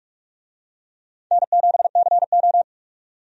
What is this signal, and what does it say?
Morse code, heard as a keyed tone, about 41 wpm, 700 Hz. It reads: N7CO
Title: USA Call Signs - 4 Characters - 45wpm